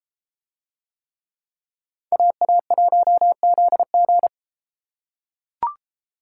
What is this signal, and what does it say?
Morse code, heard as a keyed tone, about 33 wpm, 700 Hz. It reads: AA1ZG